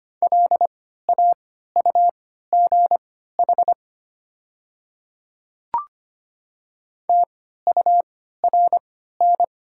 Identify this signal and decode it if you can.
Morse code, heard as a keyed tone, about 25 wpm, spaced as for 14 wpm, 700 Hz. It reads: LAUGH TURN
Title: 3x Character Spacing - Top 500 Words - 25wpm